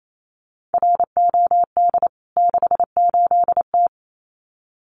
Morse code, heard as a keyed tone, about 28 wpm, 700 Hz. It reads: ROD 68T
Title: CWT Exchange 28wpm